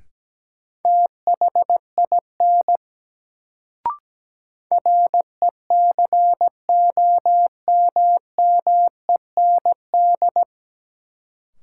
Morse code, heard as a keyed tone, about 17 wpm, 700 Hz. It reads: THIN RECOMMEND
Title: Sentences from Top 1200 Words - New Word Drill - 17wpm